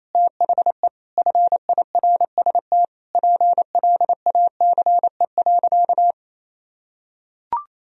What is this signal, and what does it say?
Morse code, heard as a keyed tone, about 28 wpm, 700 Hz. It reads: THE FIRST PLACE.
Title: Sentences from Top 1000 Words - Review Part 1 - 28wpm